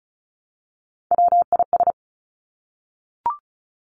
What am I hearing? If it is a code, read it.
Morse code, heard as a keyed tone, about 36 wpm, 700 Hz. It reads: WIS
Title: Three Challenging Letter-Number Combinations 35wpm